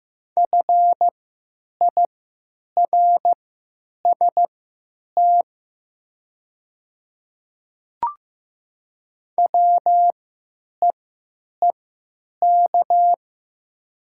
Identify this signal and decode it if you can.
Morse code, heard as a keyed tone, about 15 wpm, spaced as for 9 wpm, 700 Hz. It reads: FIRST WEEK
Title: 3x Character Spacing - Top 500 Words - 15wpm